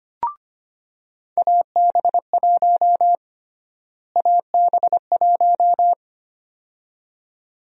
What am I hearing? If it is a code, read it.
Morse code, heard as a keyed tone, about 25 wpm, 700 Hz. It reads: AB1 AB1
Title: USA Call Sign Prefixes - Repeated 2x - 25wpm